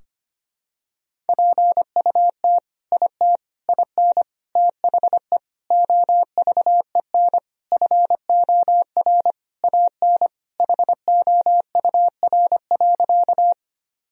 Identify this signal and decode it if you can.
Morse code, heard as a keyed tone, about 25 wpm, 700 Hz. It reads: PUT IT IN THE OVEN FOR AN HOUR.
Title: Sentences from Top 1000 Words - Review Part 4 - 25wpm